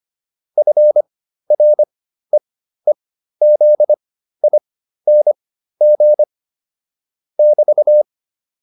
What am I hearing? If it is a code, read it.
Morse code, heard as a keyed tone, about 25 wpm, spaced as for 13 wpm, 600 Hz. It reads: FREEZING <BT>
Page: intermediate-practice-files